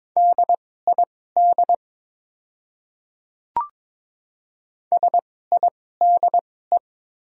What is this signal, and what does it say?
Morse code, heard as a keyed tone, about 22 wpm, spaced as for 16 wpm, 690 Hz. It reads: DID SIDE
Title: 2x Character Spacing - Top 100 Words - 22wpm